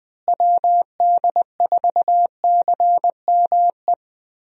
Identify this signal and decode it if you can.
Morse code, heard as a keyed tone, about 20 wpm, 700 Hz. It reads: WD4CME
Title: USA Call Signs - 6 Characters - Repeated 3x - 20wpm